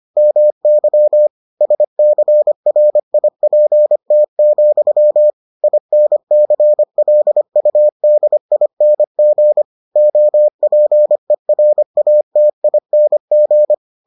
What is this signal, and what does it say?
Morse code, heard as a keyed tone, about 25 wpm, 600 Hz. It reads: MY SCRIPT, INCLUDING OPERATING